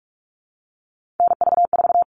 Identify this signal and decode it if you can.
Morse code, heard as a keyed tone, about 44 wpm, 700 Hz. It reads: NV4
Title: All Cut Numbers - 3-Digit 45wpm